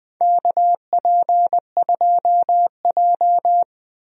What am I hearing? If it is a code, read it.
Morse code, heard as a keyed tone, about 20 wpm, 700 Hz. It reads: KP2J
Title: DXCC-100 Call Signs - 4 Characters - 20wpm